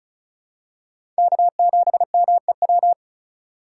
Morse code, heard as a keyed tone, about 36 wpm, 700 Hz. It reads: K7MEW
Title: USA Call Signs - 5 Characters - 2x Character Spacing - 35wpm